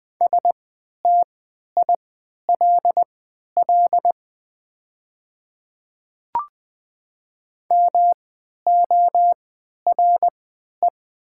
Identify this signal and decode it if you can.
Morse code, heard as a keyed tone, about 20 wpm, spaced as for 11 wpm, 690 Hz. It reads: STILL MORE
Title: 3x Character Spacing - Top 300 Words - 20wpm